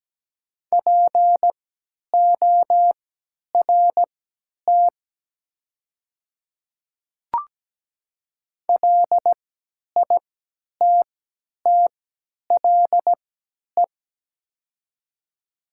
Morse code, heard as a keyed tone, about 17 wpm, spaced as for 10 wpm, 690 Hz. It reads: PORT LITTLE
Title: 3x Character Spacing - Top 500 Words - 17wpm